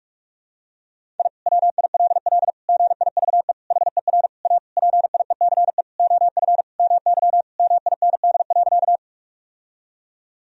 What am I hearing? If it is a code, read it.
Morse code, heard as a keyed tone, about 46 wpm, 700 Hz. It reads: I WILL GIVE HER A PIECE OF MY MIND.
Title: Sentences from Top 1000 Words - Review Part 2 - 45wpm